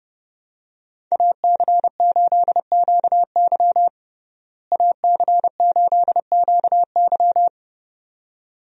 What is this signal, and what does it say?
Morse code, heard as a keyed tone, about 30 wpm, 700 Hz. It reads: AC8QY AC8QY
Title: USA Call Signs - 5 Characters - Repeated 2x - 30wpm